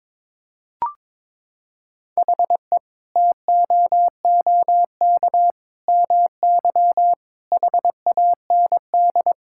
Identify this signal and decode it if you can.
Morse code, heard as a keyed tone, about 22 wpm, 700 Hz. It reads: HE TOOK MY HAND
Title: Sentences from Top 1000 Words - Review Part 6 - 22wpm